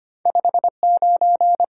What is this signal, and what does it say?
Morse code, heard as a keyed tone, about 25 wpm, 700 Hz. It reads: 5 9
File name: Top 90 QSO Elements - 8x Character Spacing - 25wpm